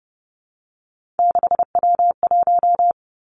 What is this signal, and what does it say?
Morse code, heard as a keyed tone, about 30 wpm, 700 Hz. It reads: 6W1
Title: Three Challenging Letter-Number Combinations 30wpm